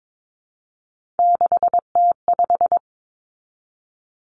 Morse code, heard as a keyed tone, about 22 wpm, 700 Hz. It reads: 6T5
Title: Three Challenging Letter-Number Combinations 22wpm